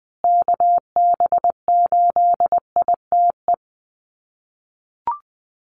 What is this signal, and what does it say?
Morse code, heard as a keyed tone, about 20 wpm, 700 Hz. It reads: KB8ITE
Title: USA Call Signs Repeated 2x 20wpm